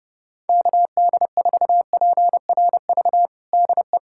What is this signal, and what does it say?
Morse code, heard as a keyed tone, about 32 wpm, 700 Hz. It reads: KD4PRV DE
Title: USA Callsign DE Callsign - 6 Characters - 30wpm